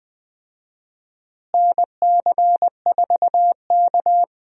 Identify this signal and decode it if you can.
Morse code, heard as a keyed tone, about 20 wpm, 700 Hz. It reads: NC4K
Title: USA Call Signs - 4 Characters - Repeated 2x - 20wpm